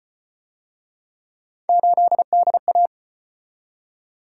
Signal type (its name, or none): Morse code